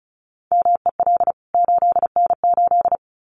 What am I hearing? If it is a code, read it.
Morse code, heard as a keyed tone, about 34 wpm, 700 Hz. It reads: MEL 8N8
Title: CWT Exchange 35wpm